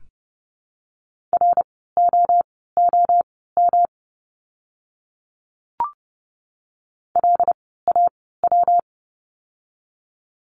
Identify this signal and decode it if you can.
Morse code, heard as a keyed tone, about 30 wpm, spaced as for 17 wpm, 700 Hz. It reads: ROOM LAW
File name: course / 30:10wpm / 115 Lesson 19f - Letter w - Words 30wpm